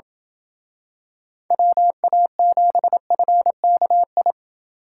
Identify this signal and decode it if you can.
Morse code, heard as a keyed tone, about 27 wpm, 700 Hz. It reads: WA7FKI